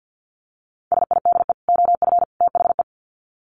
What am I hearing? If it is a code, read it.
Morse code, heard as a keyed tone, about 50 wpm, 700 Hz. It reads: SIDE OF THE
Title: Sets of 3 Words 50wpm